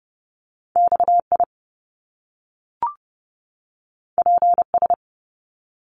Morse code, heard as a keyed tone, about 30 wpm, 700 Hz. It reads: XI PS
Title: Top Two Letter Combinations in Words 30wpm